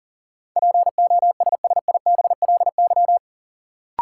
Morse code, heard as a keyed tone, about 40 wpm, 700 Hz. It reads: POSSIBLY
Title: Sentences from Top 1300 Words - New Word Drill Alt - 40wpm